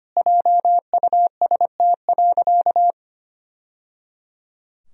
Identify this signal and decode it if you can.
Morse code, heard as a keyed tone, about 25 wpm, 700 Hz. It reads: JUST.
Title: Sentences from Top 1000 Words - Review Part 1 - 25wpm